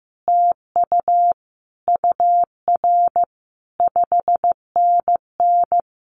Morse code, heard as a keyed tone, about 15 wpm, 700 Hz. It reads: TU UR 5NN
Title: Common QSO Phrases Encore 15wpm